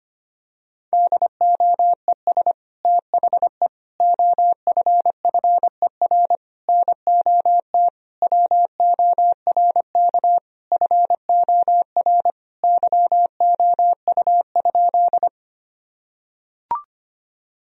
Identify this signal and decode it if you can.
Morse code, heard as a keyed tone, about 25 wpm, 700 Hz. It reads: DOES THE OFFER NOT WORK FOR YOU?
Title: Sentences from Top 1000 Words - Review Part 4 - 25wpm